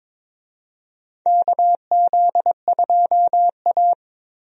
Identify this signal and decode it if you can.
Morse code, heard as a keyed tone, about 22 wpm, 700 Hz. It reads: KZ2A
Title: USA Call Signs - 4 Characters - Repeated 3x - 22wpm